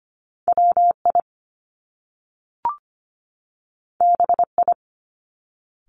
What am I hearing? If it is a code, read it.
Morse code, heard as a keyed tone, about 25 wpm, 700 Hz. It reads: WI BI
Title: Top Two Letter Combinations in Words 25wpm